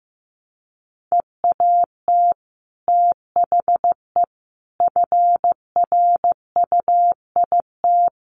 Morse code, heard as a keyed tone, about 15 wpm, 700 Hz. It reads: EAT THE FRUIT
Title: Sentences from Top 1300 Words 15wpm